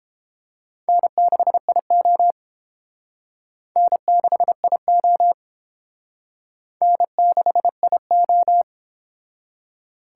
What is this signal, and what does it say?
Morse code, heard as a keyed tone, about 30 wpm, 700 Hz. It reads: N6IO N6IO N6IO